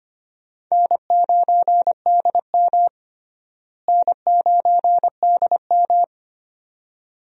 Morse code, heard as a keyed tone, about 25 wpm, 700 Hz. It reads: N9DM N9DM
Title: USA Call Signs - 4 Characters - Repeated 2x - 25wpm